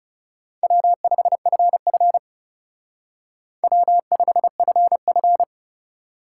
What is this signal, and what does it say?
Morse code, heard as a keyed tone, about 38 wpm, 700 Hz. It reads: W5FF W5FF